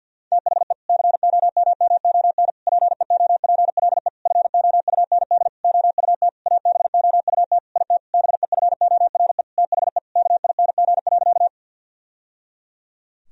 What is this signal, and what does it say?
Morse code, heard as a keyed tone, about 48 wpm, 700 Hz. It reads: THE COMMON PEOPLE FOUND OUT ABOUT IT BEFORE THE KING.
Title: Sentences from Top 1000 Words - Review Part 3 - 50wpm